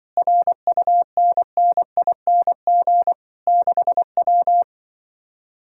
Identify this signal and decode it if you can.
Morse code, heard as a keyed tone, about 24 wpm, 700 Hz. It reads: RUNNING 6W
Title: Speed-Racing Common QSO Phrases - 20wpm